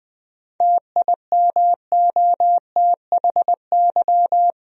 Morse code, heard as a keyed tone, about 20 wpm, 700 Hz. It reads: TIMOTHY
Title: USA Common Names - 20wpm